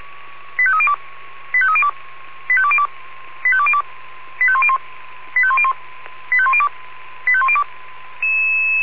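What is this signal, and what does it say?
Five-tone selective call (ZVEI), heard as quick runs of five tones, tones of about 70 ms. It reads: 85302 85302 85202 85202 85101 85101 85102 85102
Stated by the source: ZVEI five-tone call received by SDR.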